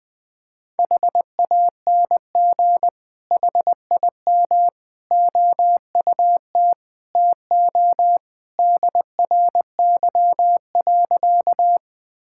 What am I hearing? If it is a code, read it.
Morse code, heard as a keyed tone, about 21 wpm, 700 Hz. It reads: HANG HIM OUT TO DRY.
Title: Sentences from Top 1600 Words 20wpm